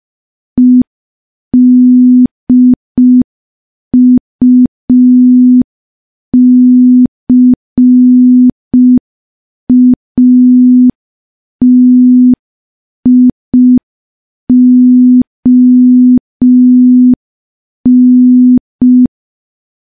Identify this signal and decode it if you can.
Morse code, heard as a keyed tone, about 5 wpm, 250 Hz. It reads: EDUCATION
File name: morse / morse-training-05WPM